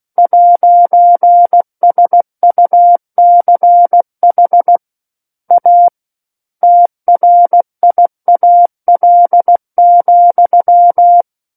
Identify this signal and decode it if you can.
Morse code, heard as a keyed tone, about 16 wpm, 700 Hz. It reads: 'SUCH A TRIAL,
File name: 03-Chapter-03 16 WPM